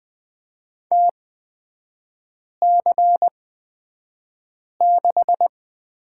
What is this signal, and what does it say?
Morse code, heard as a keyed tone, about 20 wpm, spaced as for 5 wpm, 700 Hz. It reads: TC6
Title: 243 Lesson 39e - Letter z - Three Characters 20wpm